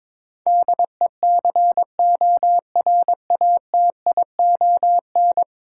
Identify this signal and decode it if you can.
Morse code, heard as a keyed tone, about 23 wpm, 700 Hz. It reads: DECORATION
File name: practice / 22wpm / 10 Letter Words - First 3 Letters Given - 22wpm